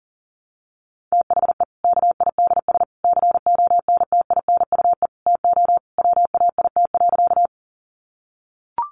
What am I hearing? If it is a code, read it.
Morse code, heard as a keyed tone, about 40 wpm, 700 Hz. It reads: THE KIDS CONTINUE TO WAIT.
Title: Easy Phrases 40wpm